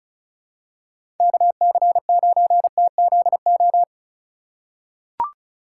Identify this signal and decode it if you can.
Morse code, heard as a keyed tone, about 34 wpm, 700 Hz. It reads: KC9TZO
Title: K1USN SST QSO Practice - 35wpm